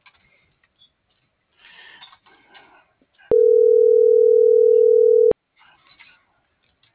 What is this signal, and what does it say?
Ringback tone (North America), 440 + 480 Hz.